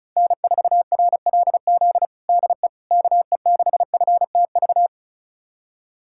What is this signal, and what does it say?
Morse code, heard as a keyed tone, about 36 wpm, 690 Hz. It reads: N4RLZ DE KE6FTV
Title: USA Callsign DE Callsign 35wpm